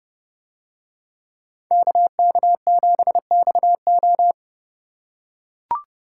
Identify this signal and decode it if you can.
Morse code, heard as a keyed tone, about 29 wpm, 700 Hz. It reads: KK7XO